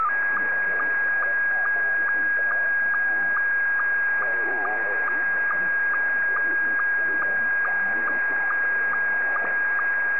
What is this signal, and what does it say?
SSTV (slow-scan television), Scottie 1, with speech.